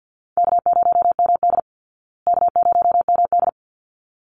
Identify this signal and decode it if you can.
Morse code, heard as a keyed tone, about 48 wpm, 700 Hz. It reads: K0MD K0MD
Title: Call Signs Repeated 3x - 50wpm